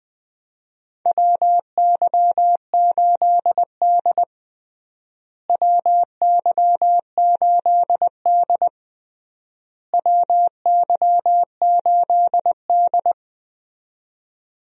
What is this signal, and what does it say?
Morse code, heard as a keyed tone, about 20 wpm, 700 Hz. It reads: WY8D WY8D WY8D
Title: USA Call Signs - 4 Characters - Repeated 3x - 20wpm